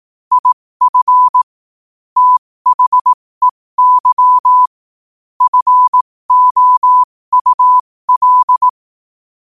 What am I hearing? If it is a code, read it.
Morse code, heard as a keyed tone, about 18 wpm, spaced as for 15 wpm, 1000 Hz. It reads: IF THEY FOUL
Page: 15 wpm morse code quotes